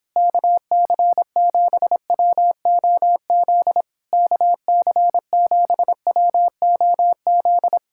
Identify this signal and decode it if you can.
Morse code, heard as a keyed tone, about 26 wpm, 700 Hz. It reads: KC7WOZ KC7WOZ